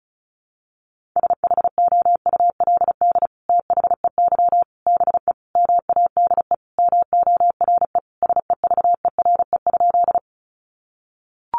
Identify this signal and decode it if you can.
Morse code, heard as a keyed tone, about 34 wpm, 700 Hz. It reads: SHOULD THEY BE MADE MORE SEVERE?
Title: Sentences from Top 1400 Words 35wpm